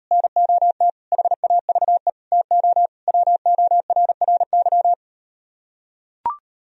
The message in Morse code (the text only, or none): NOT HAVE TO WORRY